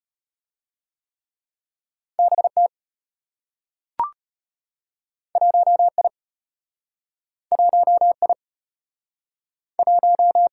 Morse code, heard as a keyed tone, about 34 wpm, 700 Hz. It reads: BT 1I 1I 1